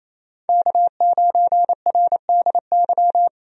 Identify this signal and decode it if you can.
Morse code, heard as a keyed tone, about 27 wpm, 700 Hz. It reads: K9RDY
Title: USA Call Signs - 5 Characters - 28wpm